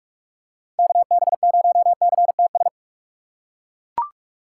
Morse code, heard as a keyed tone, about 41 wpm, 690 Hz. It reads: KB0CTS
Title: USA Call Signs - 6 Characters - 3x Character Spacing - 45wpm